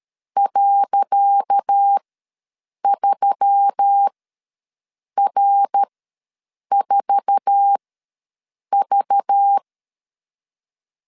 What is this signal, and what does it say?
Morse code, heard as a keyed tone, about 13 wpm, spaced as for 7 wpm, 800 Hz. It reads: .3R4V
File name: LEZIONI / 10 SIGNS / 10C DOT ws 12wpm 3548